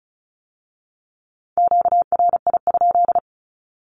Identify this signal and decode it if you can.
Morse code, heard as a keyed tone, about 36 wpm, 700 Hz. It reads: QRI?
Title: Q-Codes with Spoken Meaning - 35wpm